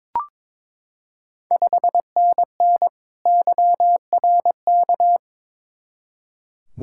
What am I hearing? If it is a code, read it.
Morse code, heard as a keyed tone, about 22 wpm, 700 Hz. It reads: 5NN YRK
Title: Ontario QSO Party Exchange 22wpm